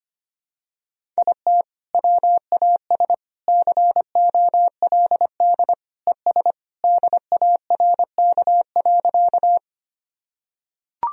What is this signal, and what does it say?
Morse code, heard as a keyed tone, about 25 wpm, 700 Hz. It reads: IT WAS COLD ES DARK.
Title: Sentences from Top 500 Words and Abbreviations 25wpm